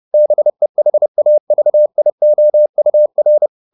Morse code, heard as a keyed tone, about 30 wpm, 600 Hz. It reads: BEHAVIOUR